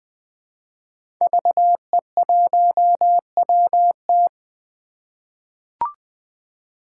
Morse code, heard as a keyed tone, about 21 wpm, 700 Hz. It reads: VE1WT
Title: 2023 SOTA Chaser Call Signs - 20wpm